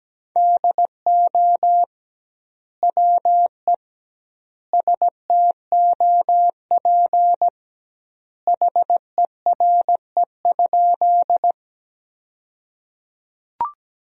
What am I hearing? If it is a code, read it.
Morse code, heard as a keyed tone, about 17 wpm, 700 Hz. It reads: DO WE STOP HERE?
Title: 2x Word-Spacing - Sentences from Top 300 Words - 17wpm